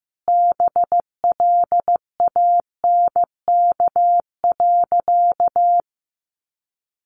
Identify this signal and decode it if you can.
Morse code, heard as a keyed tone, about 14 wpm, 700 Hz. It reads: BLANK.
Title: Sentences from Top 1500 Words 15wpm